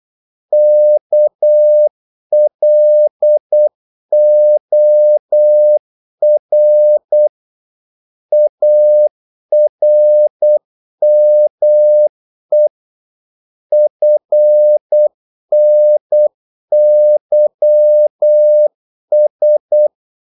Morse code, heard as a keyed tone, about 8 wpm, 600 Hz. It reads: KLOR ARME FNYS
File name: Korte ord DK 8wpm